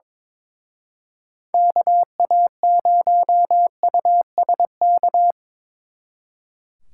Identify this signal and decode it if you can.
Morse code, heard as a keyed tone, about 25 wpm, 700 Hz. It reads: KA0USK
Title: Speed Racing - USA 2x3 Call Signs 22wpm